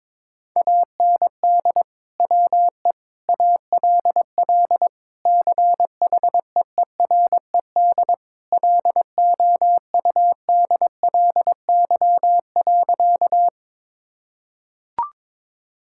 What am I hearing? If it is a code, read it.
Morse code, heard as a keyed tone, about 21 wpm, 700 Hz. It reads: AND WE ALL CHEERED LOUDLY.
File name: practice / 22wpm / Sentences with 2-Word Phrases - 22wpm